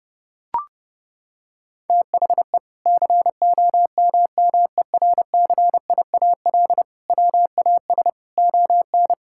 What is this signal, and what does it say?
Morse code, heard as a keyed tone, about 32 wpm, 700 Hz. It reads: THE COMMERCIAL WAS ON
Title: Sentences from Top 1000 Words - Review Part 5 - 30wpm